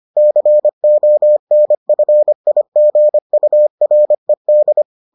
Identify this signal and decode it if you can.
Morse code, heard as a keyed tone, about 25 wpm, 600 Hz. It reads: CONFIGURED